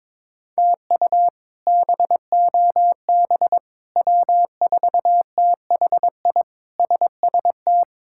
Morse code, heard as a keyed tone, about 22 wpm, 700 Hz. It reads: TU BOB W4THI SST
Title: K1USN SST QSO Practice - 22wpm